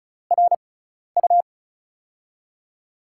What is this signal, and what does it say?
Morse code, heard as a keyed tone, about 34 wpm, spaced as for 12 wpm, 700 Hz. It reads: RU
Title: USA Call Signs - 6 Characters - 6x Character Spacing - 35wpm